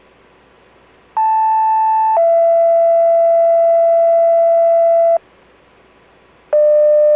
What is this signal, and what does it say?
Two-tone page, heard as a one-second tone, then a three-second tone, about 880 then 670 Hz. It is heard from 1.2 to 5.2 s.